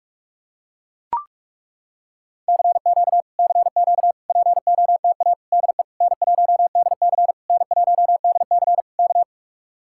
Morse code, heard as a keyed tone, about 40 wpm, 700 Hz. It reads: CQ CQ POTA DE N1DC N1DC K